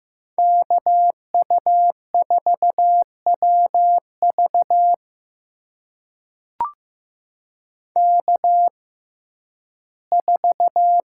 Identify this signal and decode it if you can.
Morse code, heard as a keyed tone, about 15 wpm, spaced as for 11 wpm, 700 Hz. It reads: KU4WV K4
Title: USA Call Signs - 5 Characters - 6x Character Spacing - 15wpm